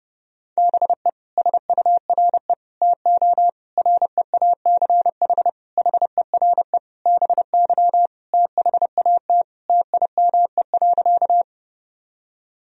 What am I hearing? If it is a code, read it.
Morse code, heard as a keyed tone, about 30 wpm, 700 Hz. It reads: BE SURE TO REACH HERE BY THAT TIME.
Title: Sentences from Top 1000 Words - Review Part 2 - 30wpm